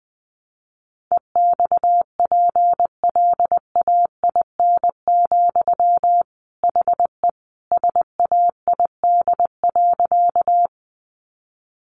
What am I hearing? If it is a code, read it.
Morse code, heard as a keyed tone, about 20 wpm, 700 Hz. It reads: EXPLAIN, HE SAID.
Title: Sentences from Top 1100 Words 20wpm